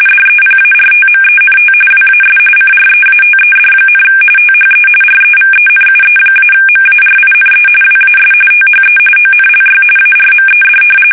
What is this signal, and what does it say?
RTTY, about 45.45 baud. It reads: DEPARTMENT WAS CREATED, THE REVENUE-MARINE WAS THE ONLY ARMED
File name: fsk850 / usnavy1